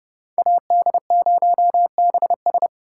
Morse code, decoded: AD0BS

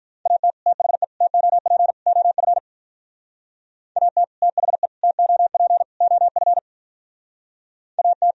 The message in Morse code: AT THE TOP OF AT THE TOP OF AT

